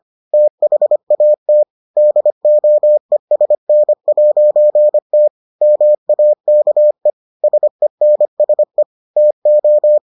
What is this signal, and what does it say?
Morse code, heard as a keyed tone, about 25 wpm, 600 Hz. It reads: THAT DOESN'T MAKE SENSE TO